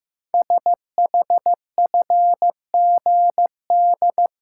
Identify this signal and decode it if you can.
Morse code, heard as a keyed tone, about 15 wpm, 700 Hz. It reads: SHFGD
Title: Code Groups - 5 Letters - 15wpm